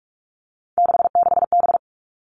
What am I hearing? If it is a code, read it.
Morse code, heard as a keyed tone, about 44 wpm, 700 Hz. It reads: 66B
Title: All Cut Numbers - 3-Digit 45wpm